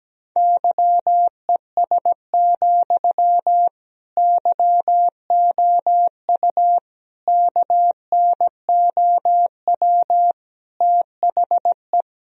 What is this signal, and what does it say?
Morse code, heard as a keyed tone, about 17 wpm, 700 Hz. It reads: YES, YOU KNOW THE
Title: Sentences from Top 1000 Words - Review Part 4 - 17wpm